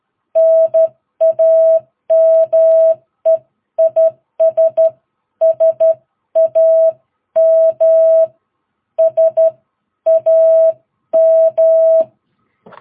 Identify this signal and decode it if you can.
Morse code, heard as a keyed tone, about 13 wpm, spaced as for 11 wpm, 660 Hz. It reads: NAME IS SAM SAM
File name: jh8cbh / 2022cw